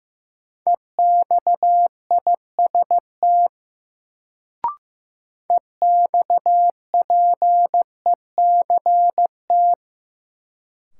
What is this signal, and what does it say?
Morse code, heard as a keyed tone, about 15 wpm, 700 Hz. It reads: EXIST EXPECT
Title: Sentences from Top 1100 Words - New Word Drill Alt - 15wpm